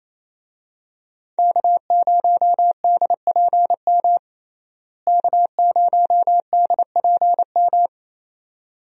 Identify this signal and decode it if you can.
Morse code, heard as a keyed tone, about 28 wpm, 700 Hz. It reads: K0DPM K0DPM
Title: USA Call Signs - 5 Characters - Repeated 2x - 28wpm